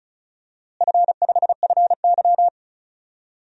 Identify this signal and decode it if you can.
Morse code, heard as a keyed tone, about 36 wpm, 700 Hz. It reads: F5FY
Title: DXCC-100 Call Signs - 4 Characters - Repeated 2x - 35wpm